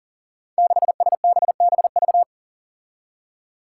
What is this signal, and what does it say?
Morse code, heard as a keyed tone, about 39 wpm, 700 Hz. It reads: 6SBBV
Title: Five Challenging Letter-Number Combinations 40wpm